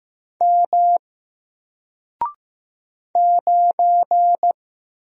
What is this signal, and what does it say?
Morse code, heard as a keyed tone, about 15 wpm, 700 Hz. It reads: M 9
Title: Single Letter-Number Alt 15wpm